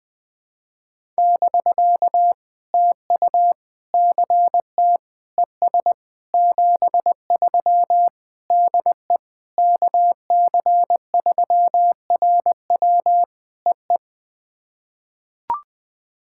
Morse code, heard as a keyed tone, about 20 wpm, 700 Hz. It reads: <BK> TU CT ES 73 DE KC3RW EE
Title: POTA QSO Practice - 20wpm